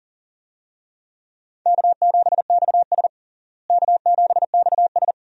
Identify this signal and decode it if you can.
Morse code, heard as a keyed tone, about 40 wpm, 700 Hz. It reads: K7XS K7XS
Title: USA Call Signs - 4 Characters - Repeated 2x - 40wpm